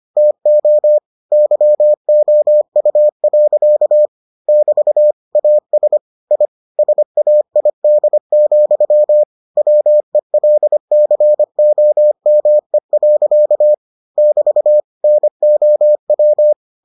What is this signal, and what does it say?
Morse code, heard as a keyed tone, about 25 wpm, 600 Hz. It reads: TO YOU. <BT> AS I SAID, WELCOME. <BT> NOW